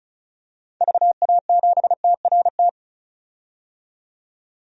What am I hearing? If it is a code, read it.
Morse code, heard as a keyed tone, about 34 wpm, 700 Hz. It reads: VA7TRT
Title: DXCC-100 Call Signs - 6 Characters - 35wpm